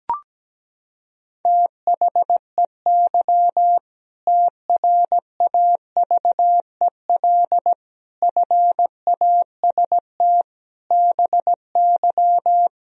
Morse code, heard as a keyed tone, about 17 wpm, 700 Hz. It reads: THEY TRAVEL FAST BY
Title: Sentences from Top 1000 Words - Review Part 3 - 17wpm